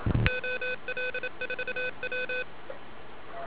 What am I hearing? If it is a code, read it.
Morse code, heard as a keyed tone, about 27 wpm, 2500 Hz, with noise. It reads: OL4W